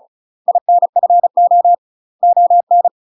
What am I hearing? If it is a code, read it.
Morse code, heard as a keyed tone, about 36 wpm, 700 Hz. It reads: INFO ON
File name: UK MASTER QSOs TRAINING / 35WPM / UK MASTER QSOs 0035